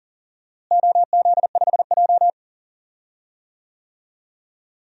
Morse code, heard as a keyed tone, about 37 wpm, 700 Hz. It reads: OZ5J
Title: DXCC-100 Call Signs - 4 Characters - Repeated 2x - 40wpm